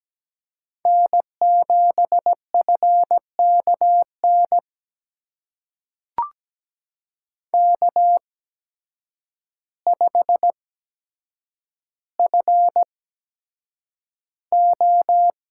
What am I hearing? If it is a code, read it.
Morse code, heard as a keyed tone, about 17 wpm, spaced as for 8 wpm, 700 Hz. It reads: N7FKN K5FO
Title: USA Call Signs - 5 Characters - 8x Character Spacing - 17wpm